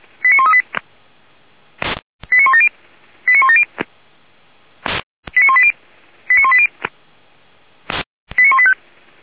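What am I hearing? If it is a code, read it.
Five-tone selective call (ZVEI), heard as quick runs of five tones, tones of about 80 ms. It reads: 89138 89170 89170 89180 89180 89185
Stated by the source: ZVEI five-tone selective call received on an SDR, hosted by Feuerwehr Orlamünde.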